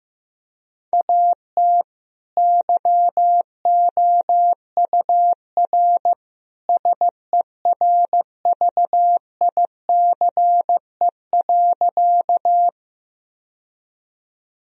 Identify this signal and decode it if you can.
Morse code, heard as a keyed tone, about 15 wpm, 700 Hz. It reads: AT YOUR SERVICE.
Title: Sentences from Top 1000 Words - Review Part 4 - 15wpm